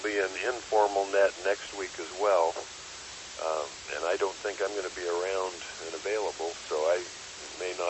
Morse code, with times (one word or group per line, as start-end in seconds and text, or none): none